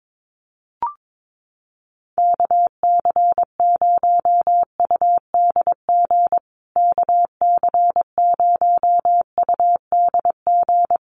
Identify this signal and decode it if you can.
Morse code, heard as a keyed tone, about 22 wpm, 700 Hz. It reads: KC0UDG KC0UDG
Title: USA Call Signs Repeated 3x 22wpm